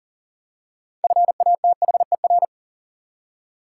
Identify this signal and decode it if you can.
Morse code, heard as a keyed tone, about 40 wpm, 700 Hz. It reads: FATHER
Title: Six Letter Words - 40wpm